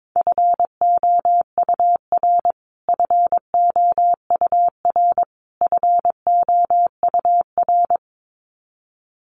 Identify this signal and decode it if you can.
Morse code, heard as a keyed tone, about 22 wpm, 700 Hz. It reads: FOUR FOUR FOUR
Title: Top 500 Words Repeated 22wpm